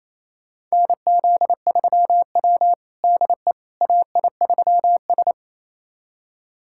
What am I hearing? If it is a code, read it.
Morse code, heard as a keyed tone, about 28 wpm, 690 Hz. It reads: NZ3W DE AI3S
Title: USA Callsign DE Callsign - 4 Characters - 28wpm